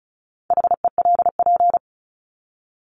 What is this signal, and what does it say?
Morse code, heard as a keyed tone, about 34 wpm, 700 Hz. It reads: HELP
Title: Top 200 Words 35wpm